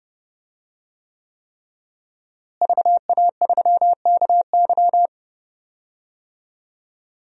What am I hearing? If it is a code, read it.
Morse code, heard as a keyed tone, about 32 wpm, 700 Hz. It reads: VA3KY
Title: DXCC-100 Call Signs - 5 Characters - Repeated 3x - 30wpm